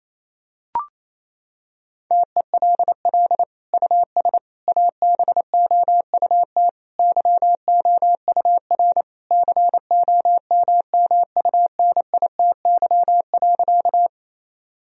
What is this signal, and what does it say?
Morse code, heard as a keyed tone, about 27 wpm, 700 Hz. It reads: TELL US ABOUT YOUR COMMUNITY.
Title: Sentences from Top 1000 Words - Review Part 4 - 28wpm